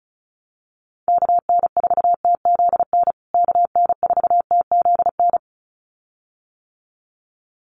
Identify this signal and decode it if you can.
Morse code, heard as a keyed tone, about 34 wpm, 700 Hz. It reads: KN4TZN KN4TZN
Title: USA Call Signs Repeated 2x 35wpm